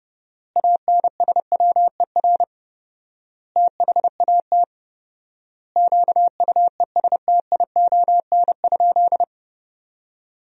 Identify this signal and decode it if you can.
Morse code, heard as a keyed tone, about 31 wpm, 700 Hz. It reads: ANSWER THAT QUESTION?
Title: 4x Word-Spacing - Sentences from Top 400 Words - 30wpm